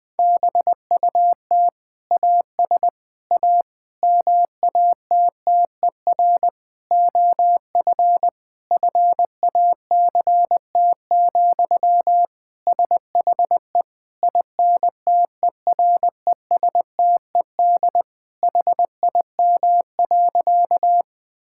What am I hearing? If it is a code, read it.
Morse code, heard as a keyed tone, about 20 wpm, 700 Hz. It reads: BUT AS A MATTER OF FACT, SHE INTERESTED HIM.
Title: Sentences from Top 1000 Words - Review Part 6 - 20wpm